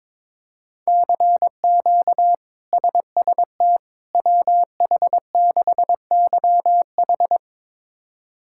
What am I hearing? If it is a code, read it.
Morse code, heard as a keyed tone, about 21 wpm, 700 Hz. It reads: CQ SST WH6YH
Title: K1USN SST QSO Practice - 22wpm